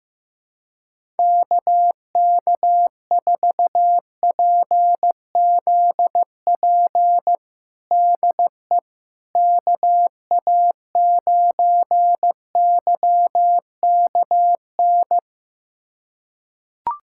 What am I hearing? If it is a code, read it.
Morse code, heard as a keyed tone, about 15 wpm, 700 Hz. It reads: KK4PZP DE KA9YKN
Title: USA Callsign DE Callsign - 6 Characters - 15wpm